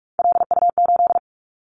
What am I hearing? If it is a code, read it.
Morse code, heard as a keyed tone, about 46 wpm, 700 Hz. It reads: LU8
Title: Callsign Prefixes for Top 100 DXCC - 45wpm